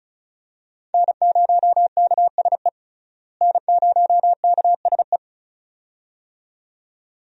Morse code, heard as a keyed tone, about 36 wpm, 700 Hz. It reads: N0KSE N0KSE
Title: USA Call Signs - 5 Characters - Repeated 2x - 35wpm